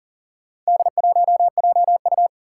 Morse code, heard as a keyed tone, about 38 wpm, 700 Hz. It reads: D1JU
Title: Four Challenging Letter-Number Combinations 40wpm